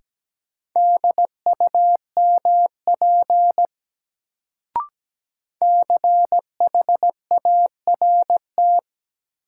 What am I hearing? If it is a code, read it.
Morse code, heard as a keyed tone, about 17 wpm, 700 Hz. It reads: DUMP CHART
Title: Sentences from Top 1300 Words - New Word Drill - 17wpm